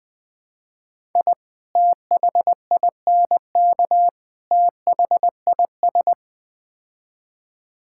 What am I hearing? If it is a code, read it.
Morse code, heard as a keyed tone, about 20 wpm, 700 Hz. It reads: I THINK THIS
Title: Build A Phrase - 4-Word Phrases - 20wpm